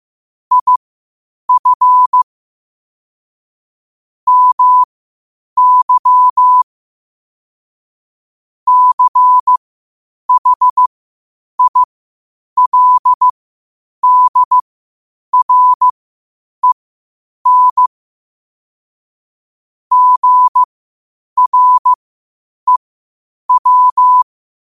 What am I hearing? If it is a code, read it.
Morse code, heard as a keyed tone, about 15 wpm, spaced as for 8 wpm, 1000 Hz. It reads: IF MY CHILDREN GREW
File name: qotdfiles / 8wpm